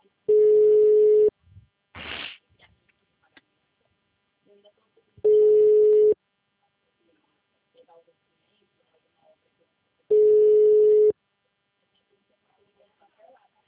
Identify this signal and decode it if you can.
Ringback tone (Europe, 425 Hz), with speech.
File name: out- -2016- - - .4900